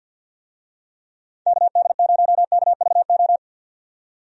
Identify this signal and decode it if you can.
Morse code, heard as a keyed tone, about 51 wpm, 700 Hz. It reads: KD0KVO
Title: USA Call Signs - 6 Characters - 50wpm